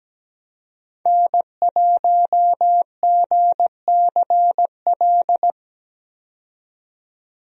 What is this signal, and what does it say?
Morse code, heard as a keyed tone, about 17 wpm, 700 Hz. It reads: N1GCL
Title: USA Call Signs - 5 Characters - 17wpm